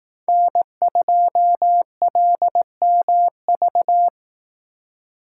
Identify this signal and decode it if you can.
Morse code, heard as a keyed tone, about 18 wpm, 700 Hz. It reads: N2LMV
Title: Speed Racing - USA 1x3 Call Signs 15wpm